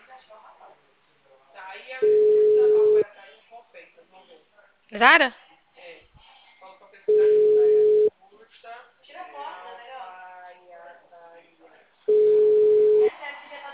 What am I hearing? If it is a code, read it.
Ringback tone (Europe, 425 Hz), with speech.